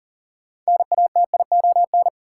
Morse code, heard as a keyed tone, about 40 wpm, 700 Hz. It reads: NATION
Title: Sentences from Top 900 Words - New Word Drill - 40wpm